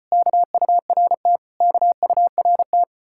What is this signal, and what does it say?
Morse code, heard as a keyed tone, about 33 wpm, 700 Hz. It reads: KURT KURT